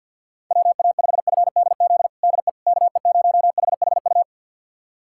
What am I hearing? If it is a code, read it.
Morse code, heard as a keyed tone, about 45 wpm, 700 Hz. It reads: WA5FDZ DE KE0HHU
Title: USA Callsign DE Callsign 50wpm